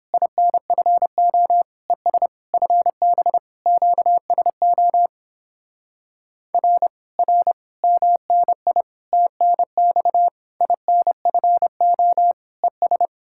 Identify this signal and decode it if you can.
Morse code, heard as a keyed tone, about 29 wpm, 700 Hz. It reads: INFO ES FB QSO R R MNI TNX INFO ES